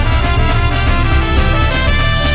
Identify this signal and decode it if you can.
There is no code.